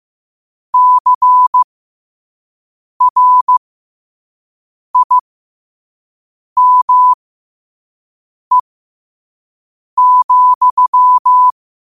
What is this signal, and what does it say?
Morse code, heard as a keyed tone, about 15 wpm, spaced as for 5 wpm, 1000 Hz. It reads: CRIME,
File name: qotdfiles / 5wpm